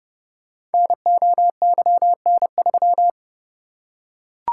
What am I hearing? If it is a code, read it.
Morse code, heard as a keyed tone, about 30 wpm, 700 Hz. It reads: NOYN3
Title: Code Groups - 5 Letter:Numbers - 30wpm